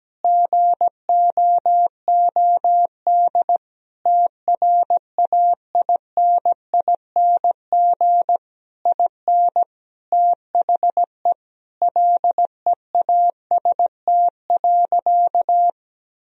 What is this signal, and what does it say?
Morse code, heard as a keyed tone, about 17 wpm, 700 Hz. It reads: GOOD TRAINING IN THE LEAST.
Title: Sentences from Top 1000 Words - Review Part 4 - 17wpm